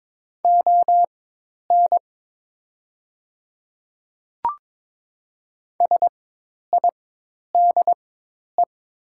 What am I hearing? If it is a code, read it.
Morse code, heard as a keyed tone, about 22 wpm, spaced as for 10 wpm, 700 Hz. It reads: ON SIDE
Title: 4x Character Spacing - Top 100 Words - 22wpm